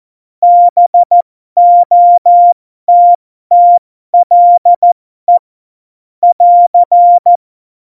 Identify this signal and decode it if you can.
Morse code, heard as a keyed tone, about 14 wpm, spaced as for 12 wpm, 700 Hz. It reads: BOTTLE <AR>
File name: Morsekurs / Witze